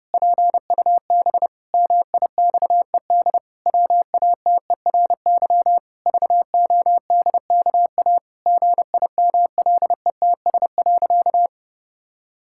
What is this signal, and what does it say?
Morse code, heard as a keyed tone, about 32 wpm, 700 Hz. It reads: PUB MIXED WATERY VODKA GIMLETS.
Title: Pangram Challenge v2 - 30wpm